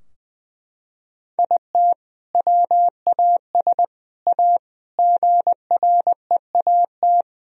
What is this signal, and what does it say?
Morse code, heard as a keyed tone, about 21 wpm, 700 Hz. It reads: IT WAS A GREAT
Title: Sentences from Top 1000 Words - Review Part 4 - 20wpm